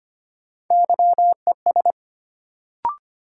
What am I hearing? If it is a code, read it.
Morse code, heard as a keyed tone, about 25 wpm, 700 Hz. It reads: YES
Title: Sentences from Top 500 Words - New Word Drill - 25wpm